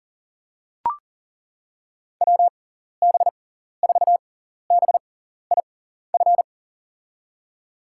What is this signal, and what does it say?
Morse code, heard as a keyed tone, about 41 wpm, spaced as for 14 wpm, 700 Hz. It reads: WB4BIF
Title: USA Call Signs - 6x Character Spacing - 40wpm